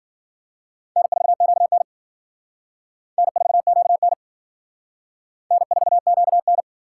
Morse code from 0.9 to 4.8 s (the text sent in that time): N4QN N4QN